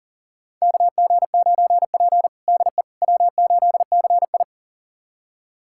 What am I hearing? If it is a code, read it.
Morse code, heard as a keyed tone, about 40 wpm, 700 Hz. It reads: KG9P DE W8CI
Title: USA Callsign DE Callsign - 4 Characters - 40wpm